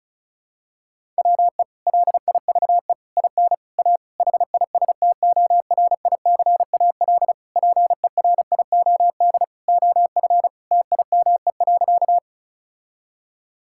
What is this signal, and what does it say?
Morse code, heard as a keyed tone, about 36 wpm, 700 Hz. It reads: WE LIVE IN A HISTORICAL PERIOD OF TIME.
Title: Sentences from Top 1000 Words - Review Part 5 - 35wpm